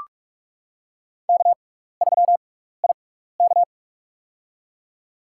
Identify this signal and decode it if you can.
Morse code, heard as a keyed tone, about 42 wpm, spaced as for 15 wpm, 700 Hz. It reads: K3IK
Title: USA Call Signs - 4 Characters - 6x Character Spacing - 45wpm